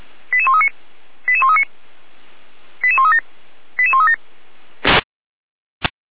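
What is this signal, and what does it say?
Five-tone selective call (ZVEI), heard as quick runs of five tones, tones of about 70 ms. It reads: 88139 88139 88137 88137
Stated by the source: ZVEI five-tone selective call received by SDR.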